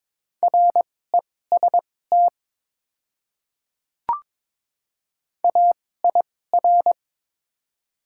Morse code, heard as a keyed tone, about 22 wpm, spaced as for 16 wpm, 700 Hz. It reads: REST AIR
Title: 2x Character Spacing - Top 500 Words - 22wpm